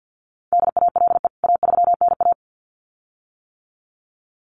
Morse code, heard as a keyed tone, about 48 wpm, 700 Hz. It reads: DALE A3NA
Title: CWT Exchange 50wpm